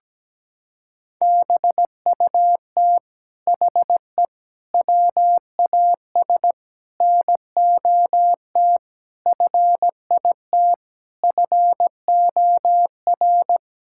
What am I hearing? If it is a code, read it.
Morse code, heard as a keyed tone, about 17 wpm, 700 Hz. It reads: BUT HE WAS NOT FIT FOR
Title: Sentences from Top 1000 Words - Review Part 6 - 17wpm